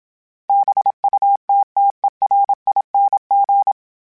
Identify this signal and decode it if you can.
Morse code, heard as a keyed tone, about 27 wpm, 800 Hz. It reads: BUTTERING
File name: Special INGs-mp3s / Special INGs-mp3s / Sngl Ing @25